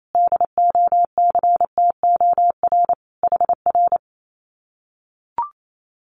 Morse code, heard as a keyed tone, about 28 wpm, 700 Hz. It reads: DOCTOR HR
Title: Common QSO Phrases Encore 28wpm